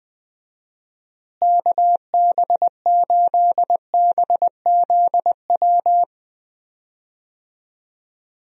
Morse code, heard as a keyed tone, about 20 wpm, 700 Hz. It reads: KB8BZW
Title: USA Call Signs - 6 Characters - Repeated 3x - 20wpm